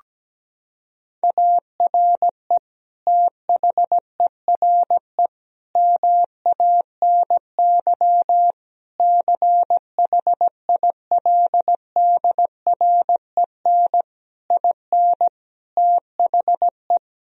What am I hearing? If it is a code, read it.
Morse code, heard as a keyed tone, about 17 wpm, 700 Hz. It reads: ARE THERE MANY CHILDREN IN THE
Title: Sentences from Top 1000 Words - Review Part 4 - 17wpm